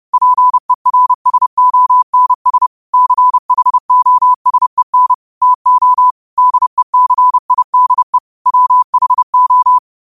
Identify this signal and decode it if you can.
Morse code, heard as a keyed tone, about 30 wpm, 1000 Hz. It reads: PERSONS CHOSEN TO DECIDE WHO
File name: qotdfiles / 30wpm